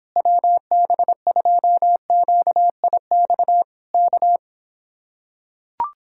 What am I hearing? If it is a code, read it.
Morse code, heard as a keyed tone, about 26 wpm, 700 Hz. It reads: WB2QIX K